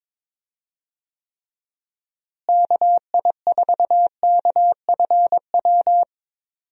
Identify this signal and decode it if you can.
Morse code, heard as a keyed tone, about 22 wpm, 700 Hz. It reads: KI4KFW
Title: USA Call Signs - 6 Characters - Repeated 3x - 22wpm